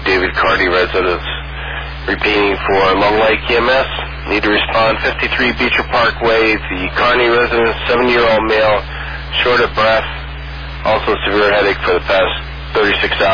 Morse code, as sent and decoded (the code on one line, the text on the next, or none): none